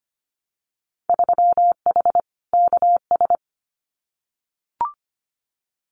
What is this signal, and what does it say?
Morse code, heard as a keyed tone, about 25 wpm, 700 Hz. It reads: 3H KS
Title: Winter Field Day Exchanges Encore 25wpm